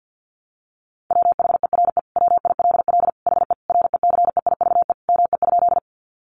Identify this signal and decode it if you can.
Morse code, heard as a keyed tone, about 50 wpm, 700 Hz. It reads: WHERE WILL HE RECEIVE ME?
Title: Sentences from Top 1000 Words 50wpm